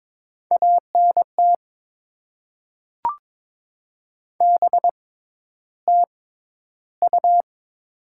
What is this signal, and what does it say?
Morse code, heard as a keyed tone, about 22 wpm, spaced as for 11 wpm, 700 Hz. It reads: ANT BTU
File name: Top 20 QSO Elements - 6x Character Spacing - 22wpm